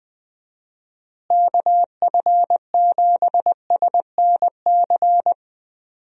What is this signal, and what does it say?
Morse code, heard as a keyed tone, about 20 wpm, 700 Hz. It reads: KF7SNC
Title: USA Call Signs - 6 Characters - Repeated 2x - 20wpm